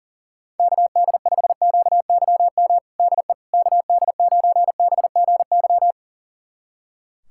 Morse code, heard as a keyed tone, about 40 wpm, 700 Hz. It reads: KD5QYM DE KD9BGY
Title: USA Callsign DE Callsign - 6 Characters - 40wpm